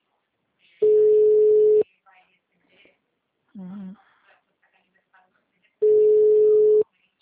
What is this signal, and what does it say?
Ringback tone (Europe, 425 Hz), with speech.
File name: out- -2033- - - .56509